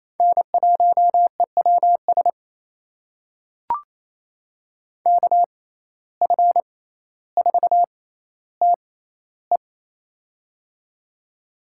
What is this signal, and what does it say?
Morse code, heard as a keyed tone, about 28 wpm, spaced as for 14 wpm, 700 Hz. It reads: N1EWS KF4TE
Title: USA Call Signs - 5 Characters - 6x Character Spacing - 28wpm